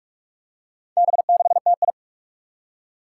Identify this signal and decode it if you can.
Morse code, heard as a keyed tone, about 44 wpm, 700 Hz. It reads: B6TI
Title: Four Challenging Letter-Number Combinations 45wpm